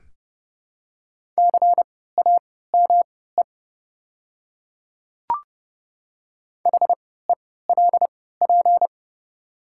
Morse code, heard as a keyed tone, about 30 wpm, spaced as for 17 wpm, 700 Hz. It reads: CAME HELP
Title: 3x Character Spacing - Top 300 Words - 30wpm